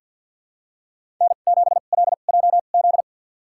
Morse code, heard as a keyed tone, about 50 wpm, spaced as for 36 wpm, 700 Hz. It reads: N7LJZ
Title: USA Call Signs - 5 Characters - 2x Character Spacing - 50wpm